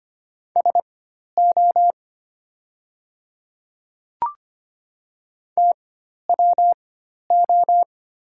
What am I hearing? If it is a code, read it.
Morse code, heard as a keyed tone, about 25 wpm, spaced as for 12 wpm, 700 Hz. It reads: SO TWO
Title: 4x Character Spacing - Top 500 Words - 25wpm